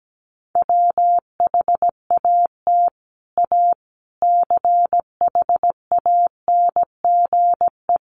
Morse code, heard as a keyed tone, about 17 wpm, 700 Hz. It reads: WHAT A CHANGE
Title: Sentences from Top 600 Words 17wpm